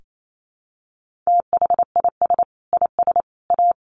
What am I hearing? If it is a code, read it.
Morse code, heard as a keyed tone, about 28 wpm, 700 Hz. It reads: THIS IS A
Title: Sentences from Top 1400 Words 28wpm